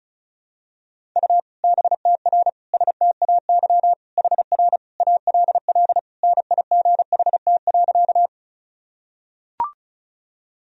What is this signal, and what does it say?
Morse code, heard as a keyed tone, about 36 wpm, 700 Hz. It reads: U BTR STAY HR ALL NIGHT.
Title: Sentences from Top 500 Words and Abbreviations 35wpm